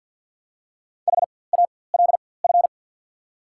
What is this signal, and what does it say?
Morse code, heard as a keyed tone, about 54 wpm, spaced as for 24 wpm, 690 Hz. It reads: HALF
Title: 4x Character Spacing - Top 500 Words - 50wpm